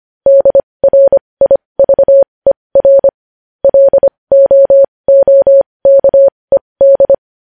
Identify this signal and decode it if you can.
Morse code, heard as a keyed tone, about 25 wpm, spaced as for 20 wpm, 550 Hz. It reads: DRIVER LOOKED
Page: News Headlines in Morse Code at 20 WPM